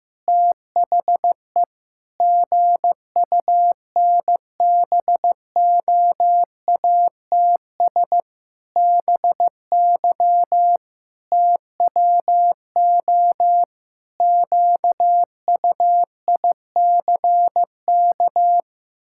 Morse code, heard as a keyed tone, about 15 wpm, 700 Hz. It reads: THE GUNBOATS BY TWO QUICK
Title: Pangram Challenge v2 - 15wpm